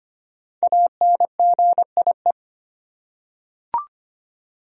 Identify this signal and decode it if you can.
Morse code, heard as a keyed tone, about 25 wpm, 700 Hz. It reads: ANGIE
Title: USA Common Names - Max 5 Letters - 25wpm